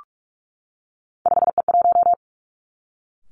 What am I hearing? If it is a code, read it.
Morse code, heard as a keyed tone, about 46 wpm, 700 Hz. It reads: 5E1
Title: Three Challenging Letter-Number Combinations 45wpm